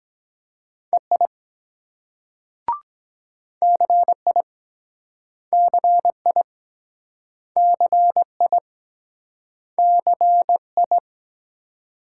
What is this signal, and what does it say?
Morse code, heard as a keyed tone, about 23 wpm, 700 Hz. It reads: EI CI CI CI CI